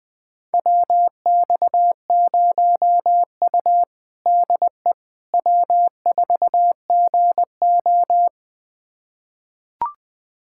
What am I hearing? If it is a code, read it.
Morse code, heard as a keyed tone, about 21 wpm, 700 Hz. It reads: WX0U DE W4GO
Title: USA Callsign DE Callsign - 4 Characters - 20wpm